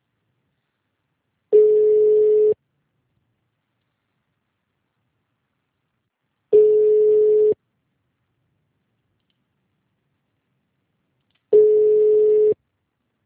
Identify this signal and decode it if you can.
Ringback tone (Europe, 425 Hz).